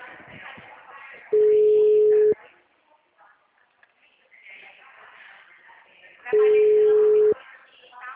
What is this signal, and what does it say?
Ringback tone (Europe, 425 Hz), with speech.